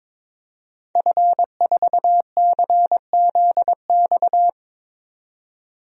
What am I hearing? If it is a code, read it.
Morse code, heard as a keyed tone, about 22 wpm, 700 Hz. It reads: F4CZX
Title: DXCC-100 Call Signs - 5 Characters - 22wpm